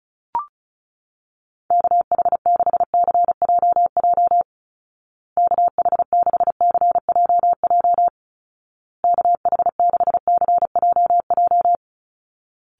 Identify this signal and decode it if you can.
Morse code, heard as a keyed tone, about 34 wpm, 700 Hz. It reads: KH6CJJ KH6CJJ KH6CJJ
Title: Call Signs Repeated 2x - 35wpm